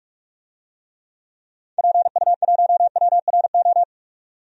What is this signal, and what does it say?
Morse code, heard as a keyed tone, about 46 wpm, 700 Hz. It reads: PU1WRO
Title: DXCC-100 Call Signs - 6 Characters - 45wpm